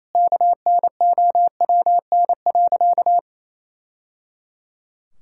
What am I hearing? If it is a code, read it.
Morse code, heard as a keyed tone, about 28 wpm, 700 Hz. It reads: KNOWN.
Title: Sentences from Top 1000 Words - Review Part 5 - 28wpm